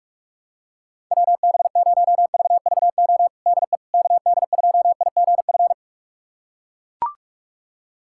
Morse code, heard as a keyed tone, about 41 wpm, 700 Hz. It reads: WB0VVO DE KD2IGF
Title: USA Callsign DE Callsign - 6 Characters - 45wpm